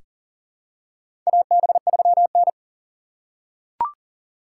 Morse code, heard as a keyed tone, about 40 wpm, 700 Hz. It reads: AB3N
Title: USA Call Signs - 4 Characters - 4x Character Spacing - 40wpm